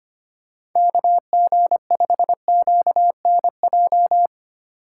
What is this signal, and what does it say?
Morse code, heard as a keyed tone, about 25 wpm, 700 Hz. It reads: KG5QNJ
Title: USA Call Signs - 4x Character Spacing - 25wpm